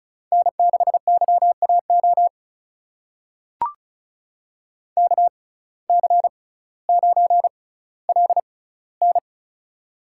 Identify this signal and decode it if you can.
Morse code, heard as a keyed tone, about 36 wpm, spaced as for 18 wpm, 700 Hz. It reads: N6YAO KC9LN
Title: USA Call Signs - 5 Characters - 6x Character Spacing - 35wpm